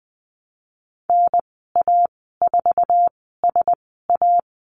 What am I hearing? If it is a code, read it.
Morse code, heard as a keyed tone, about 19 wpm, spaced as for 14 wpm, 700 Hz. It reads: NA4SA
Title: 048 Lesson 9g - Number 4 - Call Signs 20wpm